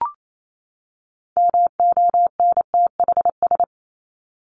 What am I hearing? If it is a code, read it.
Morse code, heard as a keyed tone, about 28 wpm, 700 Hz. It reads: MONTHS
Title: ES:S Suffix - Top 100 Words with Base Words 28wpm